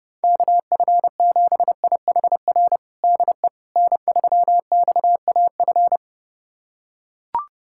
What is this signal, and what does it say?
Morse code, heard as a keyed tone, about 30 wpm, 700 Hz. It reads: KF7IHR DE N3XAF
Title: USA Callsign DE Callsign 30wpm